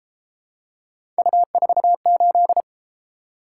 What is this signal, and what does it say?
Morse code, heard as a keyed tone, about 31 wpm, 700 Hz. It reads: U48